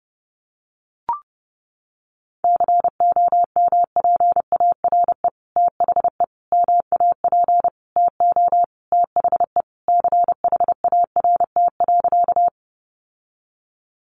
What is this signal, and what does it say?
Morse code, heard as a keyed tone, about 30 wpm, 700 Hz. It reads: COMPARE THE MAP TO THE CHART.
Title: Sentences from Top 1300 Words 30wpm